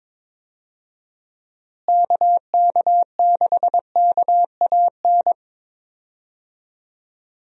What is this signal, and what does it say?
Morse code, heard as a keyed tone, about 22 wpm, 700 Hz. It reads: KK6KAN
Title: USA Call Signs - 6 Characters - Repeated 2x - 22wpm